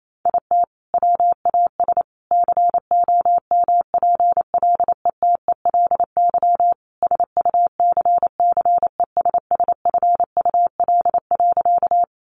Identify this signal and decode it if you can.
Morse code, heard as a keyed tone, about 28 wpm, 700 Hz. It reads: IT WAS COMPLETELY SUCCESSFUL.
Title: Sentences from Top 1100 Words 28wpm